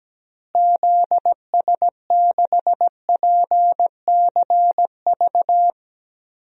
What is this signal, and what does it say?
Morse code, heard as a keyed tone, about 17 wpm, 700 Hz. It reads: ZS6PCV
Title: DXCC-100 Call Signs - 6 Characters - Repeated 2x - 17wpm